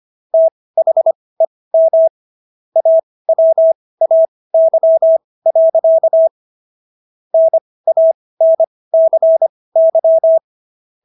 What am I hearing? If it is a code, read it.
Morse code, heard as a keyed tone, about 25 wpm, spaced as for 18 wpm, 650 Hz. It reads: THEM AWAY. NANCY